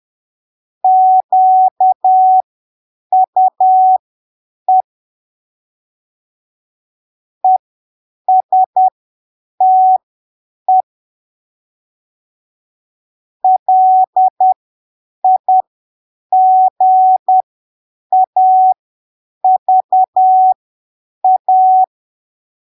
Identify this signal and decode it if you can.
Morse code, heard as a keyed tone, about 10 wpm, spaced as for 7 wpm, 750 Hz. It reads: QUE ESTE LIGAVA
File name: arquivos / texto / texto 16 10ppm